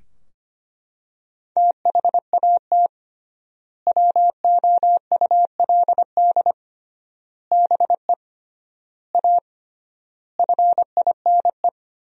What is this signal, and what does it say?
Morse code, heard as a keyed tone, about 25 wpm, 700 Hz. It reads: THAT WOULD BE A FINE
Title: 3x Word-Spacing - Sentences from Top 500 Words - 25wpm